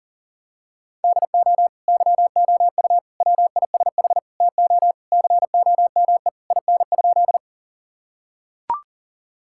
Morse code, heard as a keyed tone, about 40 wpm, 700 Hz. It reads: DO YOU WISH TO COME IN?
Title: Sentences from Top 1600 Words 40wpm